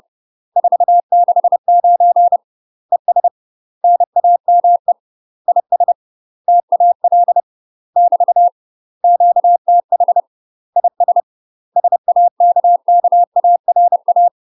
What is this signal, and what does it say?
Morse code, heard as a keyed tone, about 30 wpm, 700 Hz. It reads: 469 ES NAME IS TAL <BT> QTH IS SAKKARA
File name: UK MASTER QSOs TRAINING / 30WPM / UK MASTER QSOs 0029